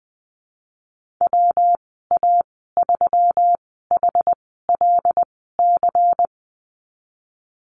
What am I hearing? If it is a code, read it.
Morse code, heard as a keyed tone, about 20 wpm, spaced as for 14 wpm, 700 Hz. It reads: WA3HLC
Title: 122 Lesson 20g - Number 3 - Call Signs 20wpm